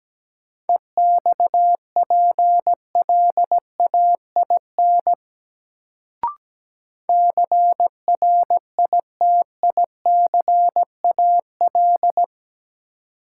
Morse code, heard as a keyed tone, about 17 wpm, 700 Hz. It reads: EXPLAIN CRITICAL
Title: Sentences from Top 1100 Words - New Word Drill - 17wpm